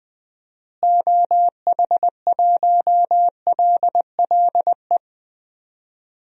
Morse code, heard as a keyed tone, about 20 wpm, 700 Hz. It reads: OH1LLE
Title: DXCC-100 Call Signs - 6 Characters - Repeated 2x - 20wpm